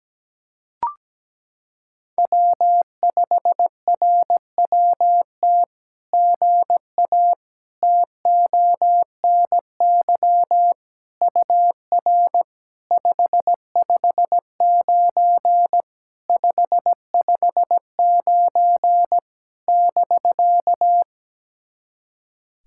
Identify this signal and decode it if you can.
Morse code, heard as a keyed tone, about 17 wpm, 700 Hz. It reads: W5RWT GA TONY UR 559 559 <BK>
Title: SOTA QSO Practice 17wpm